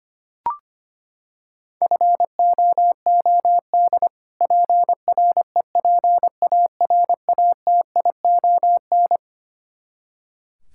Morse code, heard as a keyed tone, about 25 wpm, 700 Hz. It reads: FOOD PREPARATION
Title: Practice QSO - Exchange 3 - More Info - 25wpm